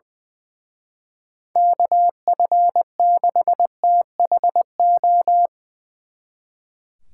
Morse code, heard as a keyed tone, about 23 wpm, 700 Hz. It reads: KF6THO
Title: Speed Racing - USA Call Signs 20wpm